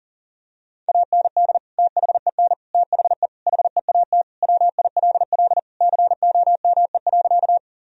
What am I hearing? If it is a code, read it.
Morse code, heard as a keyed tone, about 40 wpm, 700 Hz. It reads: AND THEN THE HEAT WILL COME.
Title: Sentences from Top 1000 Words - Review Part 3 - 40wpm